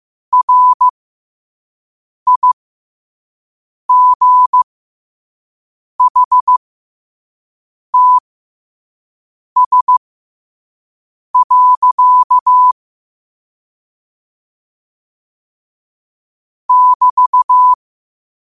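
Morse code, heard as a keyed tone, about 15 wpm, spaced as for 5 wpm, 1000 Hz. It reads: RIGHTS. <BT>
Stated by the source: Morse code quote of the day at 5 wpm.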